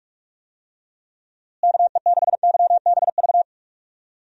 Morse code, heard as a keyed tone, about 46 wpm, 700 Hz. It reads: KE6YBV
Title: USA Call Signs - 6 Characters - Repeated 3x - 45wpm